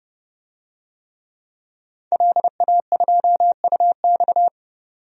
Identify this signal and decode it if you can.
Morse code, heard as a keyed tone, about 30 wpm, 700 Hz. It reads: LA2UX
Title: DXCC-100 Call Signs - 5 Characters - 30wpm